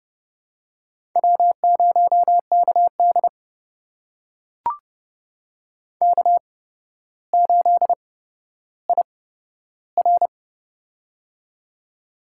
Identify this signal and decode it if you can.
Morse code, heard as a keyed tone, about 30 wpm, spaced as for 13 wpm, 700 Hz. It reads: W0KD K8IR
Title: USA Call Signs - 4 Characters - 8x Character Spacing - 30wpm